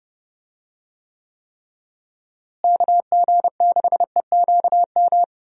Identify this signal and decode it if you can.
Morse code, heard as a keyed tone, about 30 wpm, 700 Hz. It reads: KG6EQM
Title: USA Call Signs - 6 Characters - Repeated 3x - 30wpm